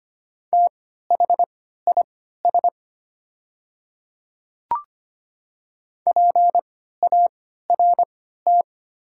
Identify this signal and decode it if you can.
Morse code, heard as a keyed tone, about 25 wpm, spaced as for 14 wpm, 700 Hz. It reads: THIS PART
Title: 3x Character Spacing - Top 200 Words - 25wpm